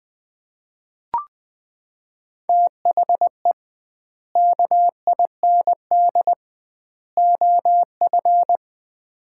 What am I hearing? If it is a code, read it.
Morse code, heard as a keyed tone, about 20 wpm, 700 Hz. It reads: THE KIND OF
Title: 2x Word-Spacing - Sets of 3 Words - 20wpm